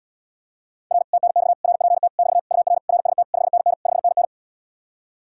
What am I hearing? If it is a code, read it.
Morse code, heard as a keyed tone, about 80 wpm, 700 Hz. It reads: R TNX RPT <BT> UR RST 5NN 5NN